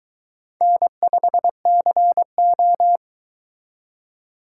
Morse code, heard as a keyed tone, about 22 wpm, 700 Hz. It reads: N5CO